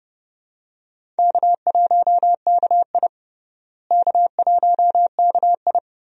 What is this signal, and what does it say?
Morse code, heard as a keyed tone, about 31 wpm, 700 Hz. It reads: K1KI K1KI
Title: USA Call Signs - 4 Characters - Repeated 2x - 30wpm